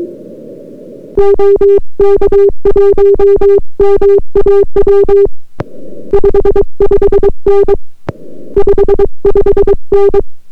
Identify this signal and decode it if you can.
Morse code, heard as a keyed tone, about 22 wpm, 400 Hz. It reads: OK1MAW 55N 55N